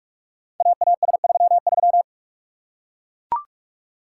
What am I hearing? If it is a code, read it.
Morse code, heard as a keyed tone, about 44 wpm, 690 Hz. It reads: AAS33
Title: Code Groups - 5 Letter:Numbers - 45wpm